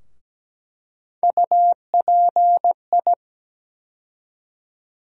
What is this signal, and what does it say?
Morse code, heard as a keyed tone, about 17 wpm, 700 Hz. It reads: UPI
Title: USA Call Sign Postfixes - 3 Letters 17wpm